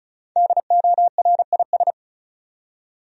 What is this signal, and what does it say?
Morse code, heard as a keyed tone, about 36 wpm, 700 Hz. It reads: DORIS
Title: USA Common Names - Max 5 Letters - 35wpm